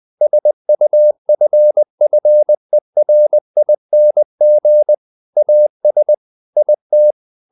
Morse code, heard as a keyed tone, about 20 wpm, 600 Hz. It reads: SUFFERING AS IT
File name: the old man and the sea / the old man and the sea part 1 20wpm